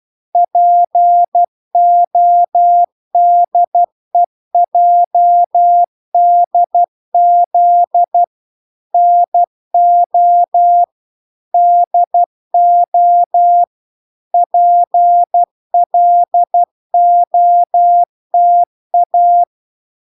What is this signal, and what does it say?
Morse code, heard as a keyed tone, about 12 wpm, 700 Hz. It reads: PODEJDZ NO DO PLOTA